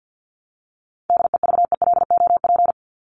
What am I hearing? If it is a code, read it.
Morse code, heard as a keyed tone, about 50 wpm, 700 Hz. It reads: DEVELOP
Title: ING Suffix - Top 100 Words with Base Words 50wpm